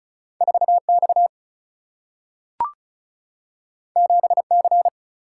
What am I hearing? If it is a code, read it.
Morse code, heard as a keyed tone, about 36 wpm, 700 Hz. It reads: 4X 7C
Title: Single-Letter Single-Number 35wpm